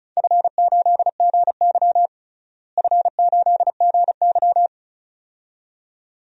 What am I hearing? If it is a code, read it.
Morse code, heard as a keyed tone, about 34 wpm, 700 Hz. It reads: F8GY F8GY
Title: DXCC-100 Call Signs - 4 Characters - Repeated 2x - 35wpm